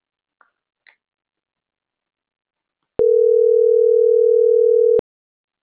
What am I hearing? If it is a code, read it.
Ringback tone (North America), 440 + 480 Hz.